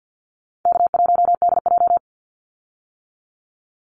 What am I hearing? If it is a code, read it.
Morse code, heard as a keyed tone, about 50 wpm, 700 Hz. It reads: K1DJ
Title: Easy Call Signs - 50wpm